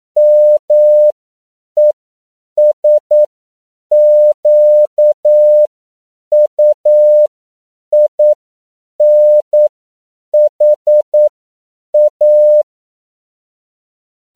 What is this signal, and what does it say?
Morse code, heard as a keyed tone, about 9 wpm, spaced as for 7 wpm, 600 Hz. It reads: MESQUINHA
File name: CW / 10PPM